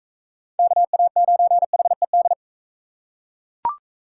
Morse code, heard as a keyed tone, about 39 wpm, 700 Hz. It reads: KA9HED